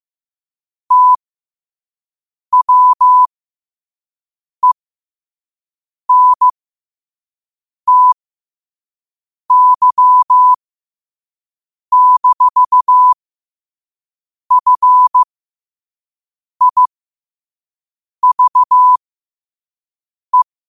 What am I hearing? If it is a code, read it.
Morse code, heard as a keyed tone, about 15 wpm, spaced as for 5 wpm, 1000 Hz. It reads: TWENTY-FIVE
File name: qotdfiles / 5wpm